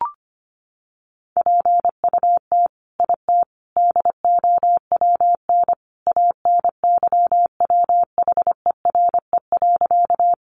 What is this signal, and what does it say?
Morse code, heard as a keyed tone, about 25 wpm, 700 Hz. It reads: PUT IT DOWN ANYWHERE.
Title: Sentences from Top 1400 Words 25wpm